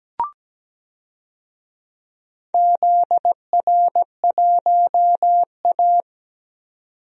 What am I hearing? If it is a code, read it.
Morse code, heard as a keyed tone, about 17 wpm, 700 Hz. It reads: ZR1A
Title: DXCC-100 Call Signs - 4 Characters - 17wpm